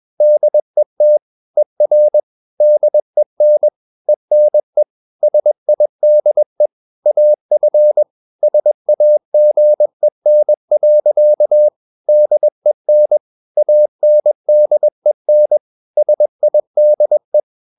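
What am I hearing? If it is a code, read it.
Morse code, heard as a keyed tone, about 21 wpm, 600 Hz. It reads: DET ER DEN ENE SIDE AF SAGEN. DEN ANDEN SIDE
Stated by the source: Morse practice at 21 wpm, slow-speed holiday exercise.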